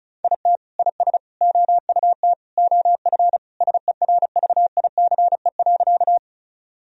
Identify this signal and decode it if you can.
Morse code, heard as a keyed tone, about 34 wpm, 700 Hz. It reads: IT IS OUT OF SERVICE.
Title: Sentences from Top 1000 Words - Review Part 4 - 35wpm